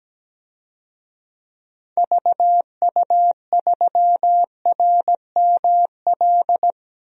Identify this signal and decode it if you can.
Morse code, heard as a keyed tone, about 17 wpm, 700 Hz. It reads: VU3RML
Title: DXCC-100 Call Signs - 6 Characters - 17wpm